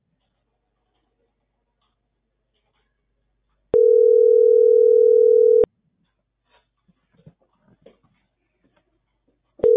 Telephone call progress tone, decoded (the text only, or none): ringback tone